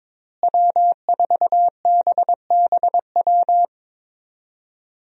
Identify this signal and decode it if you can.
Morse code, heard as a keyed tone, about 21 wpm, 700 Hz. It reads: W4BBW
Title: USA Call Signs - 5 Characters - Repeated 3x - 22wpm